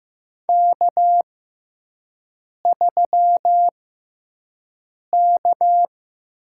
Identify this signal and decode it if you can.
Morse code, heard as a keyed tone, about 15 wpm, spaced as for 5 wpm, 700 Hz. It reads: K3K
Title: USA Call Signs - 5 Characters - 6x Character Spacing - 15wpm